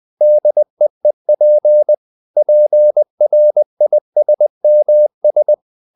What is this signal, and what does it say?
Morse code, heard as a keyed tone, about 20 wpm, 600 Hz. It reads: DEEP PRISMS
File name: the old man and the sea / the old man and the sea part 1 20wpm